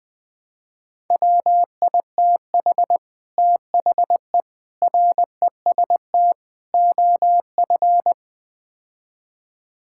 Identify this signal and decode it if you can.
Morse code, heard as a keyed tone, about 20 wpm, 700 Hz. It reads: WITH THE REST OF
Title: Build A Phrase - 5-Word Phrases - 20wpm